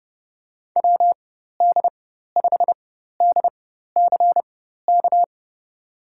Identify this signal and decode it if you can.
Morse code, heard as a keyed tone, about 32 wpm, spaced as for 14 wpm, 700 Hz. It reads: WD5DCK
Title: USA Call Signs - 6 Characters - 4x Character Spacing - 30wpm